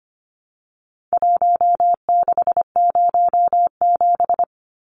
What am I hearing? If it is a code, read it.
Morse code, heard as a keyed tone, about 25 wpm, 700 Hz. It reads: 1607
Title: Numbers - Contest Style - 4-Digit 25wpm